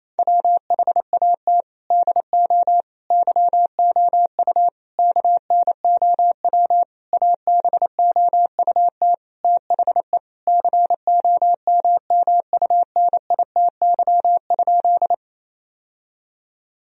Morse code, heard as a keyed tone, about 28 wpm, 700 Hz. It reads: WHAT DO YOU KNOW ABOUT THE COMMUNITY?
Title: Sentences from Top 1000 Words - Review Part 4 - 28wpm